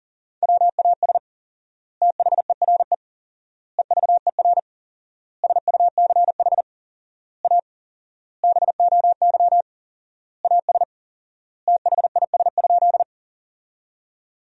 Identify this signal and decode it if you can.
Morse code, heard as a keyed tone, about 40 wpm, 700 Hz. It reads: WAS THERE EVER SUCH A BOY AS THIS?
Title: 4x Word-Spacing - Sentences from Top 400 Words - 40wpm